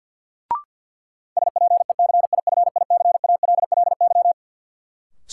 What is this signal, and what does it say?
Morse code, heard as a keyed tone, about 46 wpm, 700 Hz. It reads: SPECIFICALLY
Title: Sentences from Top 1100 Words - New Word Drill - 50wpm